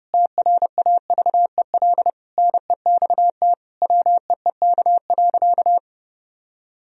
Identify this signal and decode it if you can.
Morse code, heard as a keyed tone, about 30 wpm, 700 Hz. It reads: TRAVEL NEXT WEEK.
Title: Sentences from Top 1000 Words - Review Part 2 - 30wpm